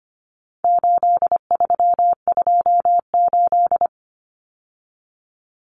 Morse code, heard as a keyed tone, about 25 wpm, 700 Hz. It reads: 8328
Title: Numbers - 4-Digit 25wpm